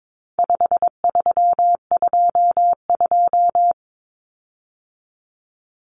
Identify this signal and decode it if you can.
Morse code, heard as a keyed tone, about 22 wpm, 700 Hz. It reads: 5322
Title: Common Cut Numbers - 4-Digit 22wpm